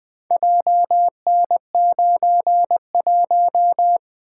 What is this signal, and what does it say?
Morse code, heard as a keyed tone, about 24 wpm, 700 Hz. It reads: JN91
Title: Speed-Racing - Four Letter-Number Combinations - 20wpm